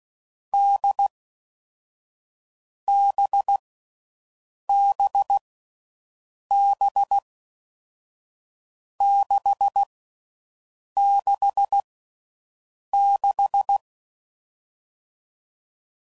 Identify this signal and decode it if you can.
Morse code, heard as a keyed tone, about 16 wpm, 800 Hz. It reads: D B B B 6 6 6
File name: Mixed / Short Runs / NDB NDB6